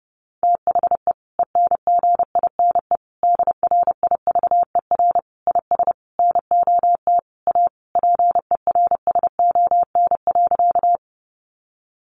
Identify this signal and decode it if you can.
Morse code, heard as a keyed tone, about 30 wpm, 700 Hz. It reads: THE ENGINE DRIVER IS NOT A PERSON.
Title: Sentences from Top 1500 Words 30wpm